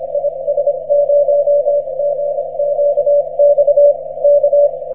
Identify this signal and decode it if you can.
Morse code, heard as a keyed tone, about 25 wpm, 580 Hz. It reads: IS0GQX K
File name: wp / is0gqx 6m cw